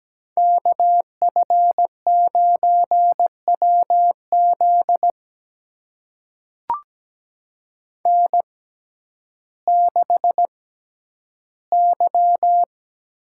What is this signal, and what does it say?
Morse code, heard as a keyed tone, about 17 wpm, spaced as for 10 wpm, 700 Hz. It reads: KF9WZ N6Y
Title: USA Call Signs - 5 Characters - 6x Character Spacing - 17wpm